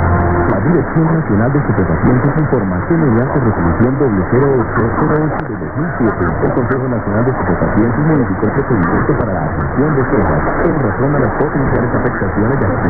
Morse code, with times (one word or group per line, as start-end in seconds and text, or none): none